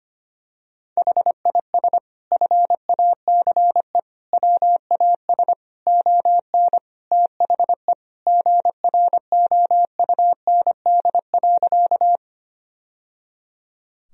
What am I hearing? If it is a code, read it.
Morse code, heard as a keyed tone, about 25 wpm, 700 Hz. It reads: HIS FACE WAS ON THE GROUND.
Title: Sentences from Top 1000 Words - Review Part 2 - 25wpm